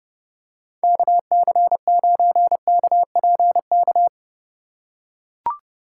Morse code, heard as a keyed tone, about 30 wpm, 700 Hz. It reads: KC9KPK